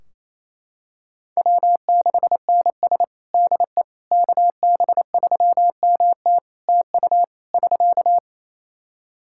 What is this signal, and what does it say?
Morse code, heard as a keyed tone, about 28 wpm, 700 Hz. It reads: W6NS DE KB3MT TU <SK>
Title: Practice QSO - Exchange 4 - Ending - 28wpm